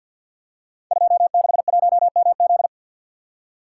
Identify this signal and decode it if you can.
Morse code, heard as a keyed tone, about 51 wpm, 700 Hz. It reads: 261MZ
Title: Code Groups - 5 Letter:Numbers - 50wpm